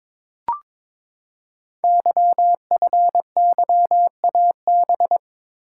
Morse code, heard as a keyed tone, about 22 wpm, 700 Hz. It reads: YFYAB
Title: Code Groups - 5 Letter:Numbers - 22wpm